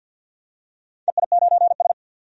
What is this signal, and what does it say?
Morse code, heard as a keyed tone, about 53 wpm, 700 Hz. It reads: EI9S
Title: DXCC-100 Call Signs - 4 Characters - 50wpm